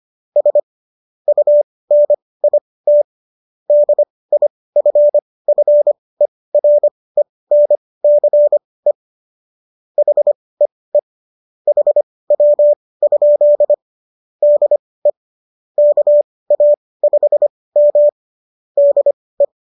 Morse code, heard as a keyed tone, about 25 wpm, spaced as for 18 wpm, 590 Hz. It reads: S UNIT DIFFERENCE HEE HW? DE KA5M DE